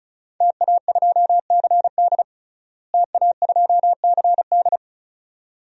Morse code, heard as a keyed tone, about 34 wpm, 700 Hz. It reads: TA2CD TA2CD
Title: DXCC-100 Call Signs - 5 Characters - Repeated 2x - 35wpm